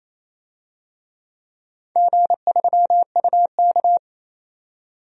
Morse code, heard as a keyed tone, about 28 wpm, 700 Hz. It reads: G3UK
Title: DXCC-100 Call Signs - 4 Characters - 28wpm